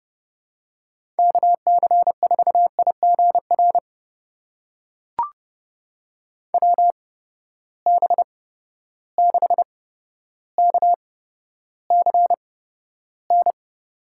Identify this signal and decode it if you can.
Morse code, heard as a keyed tone, about 32 wpm, spaced as for 13 wpm, 700 Hz. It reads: KC4IGR WB6KCN
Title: USA Call Signs - 6 Characters - 8x Character Spacing - 30wpm